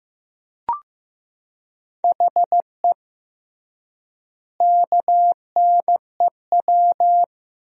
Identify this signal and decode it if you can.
Morse code, heard as a keyed tone, about 15 wpm, 700 Hz. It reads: HE KNEW
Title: 3x Word-Spacing - Sentences from Top 400 Words - 15wpm